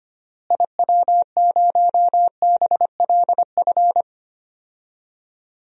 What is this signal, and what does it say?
Morse code, heard as a keyed tone, about 25 wpm, 700 Hz. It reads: IW0BLF
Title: DXCC-100 Call Signs - 6 Characters - Repeated 3x - 25wpm